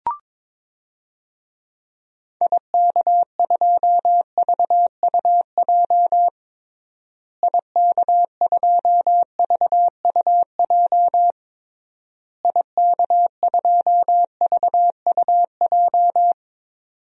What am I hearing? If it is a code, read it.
Morse code, heard as a keyed tone, about 22 wpm, 700 Hz. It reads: IK2VUJ IK2VUJ IK2VUJ
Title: DXCC-100 Call Signs - 6 Characters - Repeated 3x - 22wpm